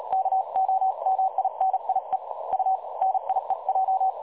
Morse code, heard as a keyed tone, about 36 wpm, 760 Hz. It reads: KOMINIE U NIEJ
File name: dlazaawansowanych / nagraniazpasma / websdr recording 3565.0kHz